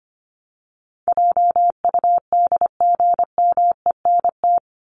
Morse code, heard as a keyed tone, about 25 wpm, 700 Hz. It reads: JUDGMENT
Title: Top 1751-2000 Words 25wpm